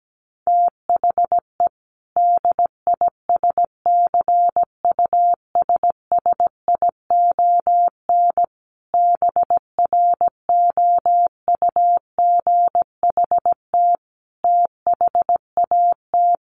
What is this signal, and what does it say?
Morse code, heard as a keyed tone, about 17 wpm, 700 Hz. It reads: THE DISCUSSION BROUGHT THAT
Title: Sentences from Top 1100 Words 17wpm